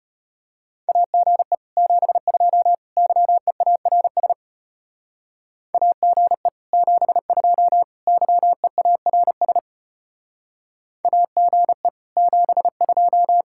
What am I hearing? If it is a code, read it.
Morse code, heard as a keyed tone, about 32 wpm, 700 Hz. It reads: AGE 72 YEARS AGE 72 YEARS AGE 72